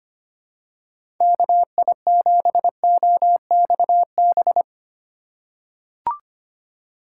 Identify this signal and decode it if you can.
Morse code, heard as a keyed tone, about 25 wpm, 700 Hz. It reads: KI7OXB
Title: USA Call Signs - 6 Characters - 4x Character Spacing - 25wpm